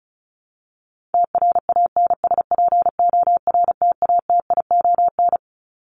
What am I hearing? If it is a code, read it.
Morse code, heard as a keyed tone, about 34 wpm, 700 Hz. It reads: TRANSPORTATION
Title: Top 1501-1750 Words 35wpm